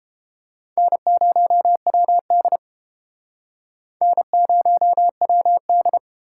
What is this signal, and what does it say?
Morse code, heard as a keyed tone, about 31 wpm, 700 Hz. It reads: N0WD N0WD